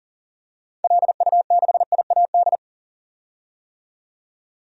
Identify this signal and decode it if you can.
Morse code, heard as a keyed tone, about 40 wpm, 700 Hz. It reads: LU6IAD
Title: DXCC-100 Call Signs - 6 Characters - 40wpm